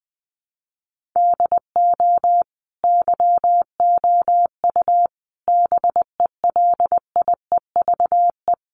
Morse code, heard as a keyed tone, about 20 wpm, 700 Hz. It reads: DO YOU BELIEVE
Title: Sentences from Top 1500 Words 20wpm